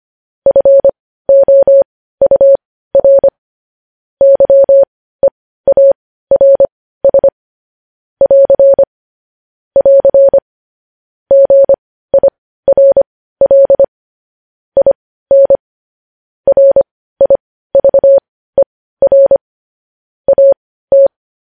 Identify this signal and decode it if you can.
Morse code, heard as a keyed tone, about 25 wpm, spaced as for 15 wpm, 550 Hz. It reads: FOUR YEARS <AR> <AR> GIRL IN RIVER AT
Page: News Headlines in Morse Code at 15 WPM